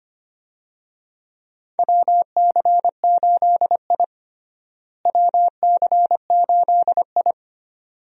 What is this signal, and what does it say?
Morse code, heard as a keyed tone, about 25 wpm, 700 Hz. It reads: WC8I WC8I
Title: USA Call Signs - 4 Characters - Repeated 2x - 25wpm